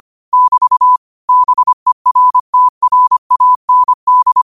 Morse code, heard as a keyed tone, about 25 wpm, 1000 Hz. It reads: <BT> BERTRAND
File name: qotdfiles / 25wpm